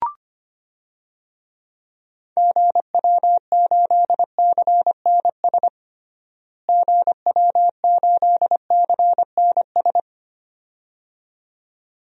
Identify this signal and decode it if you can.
Morse code, heard as a keyed tone, about 25 wpm, 700 Hz. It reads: GW8CNS GW8CNS
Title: DXCC-100 Call Signs - 6 Characters - Repeated 2x - 25wpm